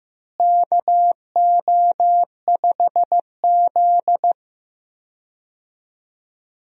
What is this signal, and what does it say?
Morse code, heard as a keyed tone, about 15 wpm, 700 Hz. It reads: KO5Z
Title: USA Call Signs - 4 Characters - Repeated 3x - 15wpm